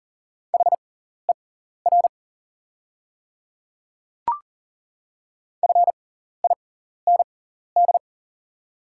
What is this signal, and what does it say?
Morse code, heard as a keyed tone, about 40 wpm, spaced as for 14 wpm, 690 Hz. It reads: HER FIND
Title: 6x Character Spacing - Top 500 Words - 40wpm